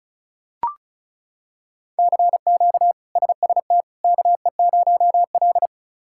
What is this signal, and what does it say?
Morse code, heard as a keyed tone, about 34 wpm, 700 Hz. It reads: CQ SST KE0L
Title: K1USN SST QSO Practice - 35wpm